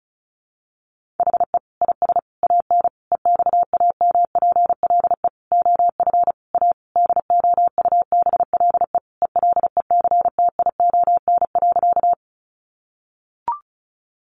Morse code, heard as a keyed tone, about 34 wpm, 700 Hz. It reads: HE IS AN EXAMPLE OF A DOUBLE ELECTION.
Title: Sentences from Top 1500 Words 35wpm